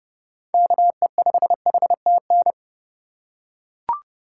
Morse code, heard as a keyed tone, about 32 wpm, 700 Hz. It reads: KE5HTN
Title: USA Call Signs - 4x Character Spacing - 30wpm